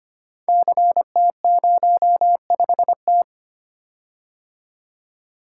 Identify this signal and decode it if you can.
Morse code, heard as a keyed tone, about 25 wpm, 700 Hz. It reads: CT05T
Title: Code Groups - 5 Letter:Numbers - 25wpm